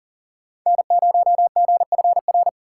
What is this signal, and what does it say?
Morse code, heard as a keyed tone, about 40 wpm, 700 Hz. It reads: N0GFR
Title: USA Call Signs - 5 Characters - 4x Character Spacing - 40wpm